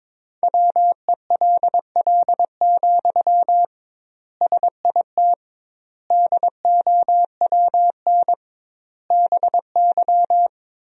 Morse code, heard as a keyed tone, about 22 wpm, 700 Hz. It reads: WELL, SIT DOWN BY
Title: 2x Word-Spacing - Sentences from Top 500 Words - 22wpm